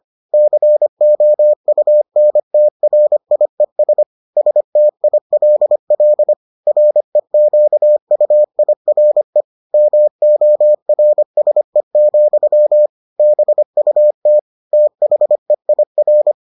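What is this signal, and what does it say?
Morse code, heard as a keyed tone, about 25 wpm, 600 Hz. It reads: COUNTRIES STILL REQUIRE MORSE, BUT THEIR
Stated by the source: Morse edition of the Foundations of Amateur Radio episodes.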